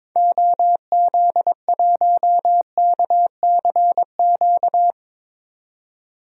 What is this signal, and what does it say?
Morse code, heard as a keyed tone, about 22 wpm, 700 Hz. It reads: OZ1KCQ
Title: DXCC-100 Call Signs - 6 Characters - 22wpm